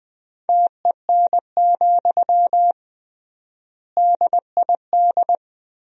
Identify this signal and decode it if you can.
Morse code, heard as a keyed tone, about 21 wpm, 700 Hz. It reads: TEN, DID
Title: 3x Word-Spacing - Sentences from Top 400 Words - 20wpm